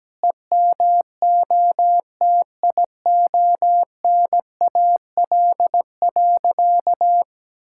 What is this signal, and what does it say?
Morse code, heard as a keyed tone, about 17 wpm, 700 Hz. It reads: EMOTIONAL.
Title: Sentences from Top 1000 Words - Review Part 5 - 17wpm